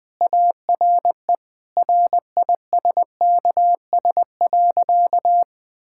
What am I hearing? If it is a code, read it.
Morse code, heard as a keyed tone, about 20 wpm, 700 Hz. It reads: ARE RISKS.
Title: Sentences with 5-Word Phrases - Spoken Beginning - 20wpm